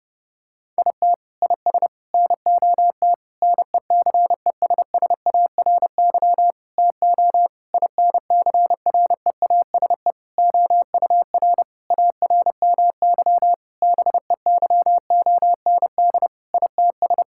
Morse code, heard as a keyed tone, about 30 wpm, 700 Hz. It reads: IT IS NOT NECESSARY TO INCREASE OUR ARMY BEYOND ITS
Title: Sentences from Top 1000 Words - Review Part 6 - 30wpm